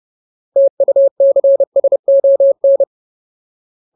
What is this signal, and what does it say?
Morse code, heard as a keyed tone, about 30 wpm, 550 Hz. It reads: TUCSON
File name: Ultimate Word List / 30WPM Ultimate